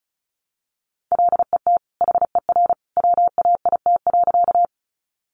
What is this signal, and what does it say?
Morse code, heard as a keyed tone, about 34 wpm, 700 Hz. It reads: LET HER WAIT.
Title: Sentences from Top 500 Words 35wpm